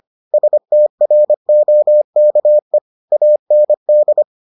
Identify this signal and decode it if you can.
Morse code, heard as a keyed tone, about 25 wpm, 600 Hz. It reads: STROKE AND